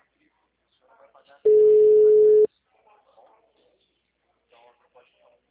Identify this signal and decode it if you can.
Ringback tone (Europe, 425 Hz), with speech.
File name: out- -2013- - - .9050